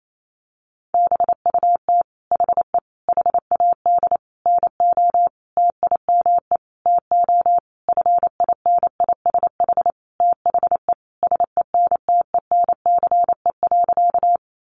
Morse code, heard as a keyed tone, about 28 wpm, 700 Hz. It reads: BUT HE HAD NO TIME TO FINISH THE SENTENCE.
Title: Sentences from Top 1200 Words 28wpm